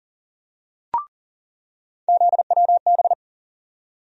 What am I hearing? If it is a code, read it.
Morse code, heard as a keyed tone, about 40 wpm, 700 Hz. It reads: ZWB
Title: USA Call Sign Postfixes - 3 Letters 40wpm